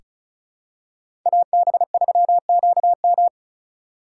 Morse code, heard as a keyed tone, about 36 wpm, 700 Hz. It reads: AB3QM
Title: USA Call Signs - 5 Characters - 4x Character Spacing - 35wpm